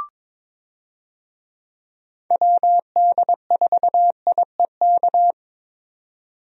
Morse code, heard as a keyed tone, about 22 wpm, 700 Hz. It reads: WD4IEK
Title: USA Call Signs - 6 Characters - 22wpm